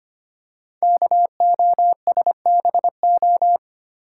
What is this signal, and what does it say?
Morse code, heard as a keyed tone, about 25 wpm, 700 Hz. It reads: KOSBO
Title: Code Groups - 5 Letters - 25wpm